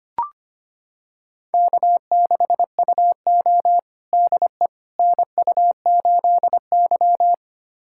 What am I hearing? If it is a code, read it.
Morse code, heard as a keyed tone, about 25 wpm, 700 Hz. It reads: K6UO DE NU8Y
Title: USA Callsign DE Callsign - 4 Characters - 25wpm